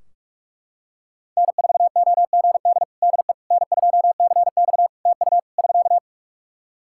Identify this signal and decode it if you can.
Morse code, heard as a keyed tone, about 41 wpm, 700 Hz. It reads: N4OGD DE N2CX TU <SK>
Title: Practice QSO - Exchange 4 - Ending - 45wpm